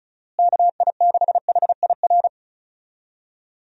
Morse code, heard as a keyed tone, about 34 wpm, 700 Hz. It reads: KI6HIR
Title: USA Call Signs - 6 Characters - Repeated 3x - 35wpm